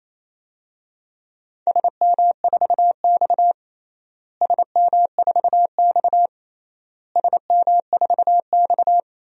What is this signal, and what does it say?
Morse code, heard as a keyed tone, about 27 wpm, 700 Hz. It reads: SM4X SM4X SM4X
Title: DXCC-100 Call Signs - 4 Characters - Repeated 3x - 28wpm